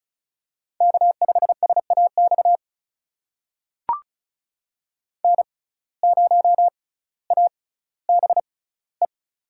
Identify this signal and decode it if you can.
Morse code, heard as a keyed tone, about 34 wpm, spaced as for 18 wpm, 700 Hz. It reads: K5SAX N0ABE
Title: USA Call Signs - 5 Characters - 6x Character Spacing - 35wpm